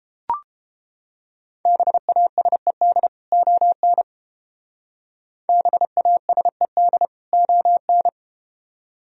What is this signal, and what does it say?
Morse code, heard as a keyed tone, about 32 wpm, 700 Hz. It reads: BASED ON BASED ON